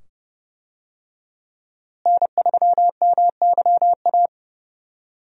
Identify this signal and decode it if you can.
Morse code, heard as a keyed tone, about 30 wpm, 700 Hz. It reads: N3MYA
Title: USA Call Signs - 5 Characters - Repeated 2x - 30wpm